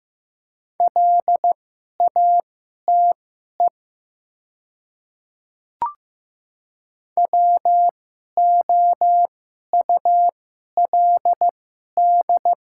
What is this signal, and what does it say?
Morse code, heard as a keyed tone, about 15 wpm, spaced as for 11 wpm, 690 Hz. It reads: LATE WOULD
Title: 2x Character Spacing - Top 300 Words - 15wpm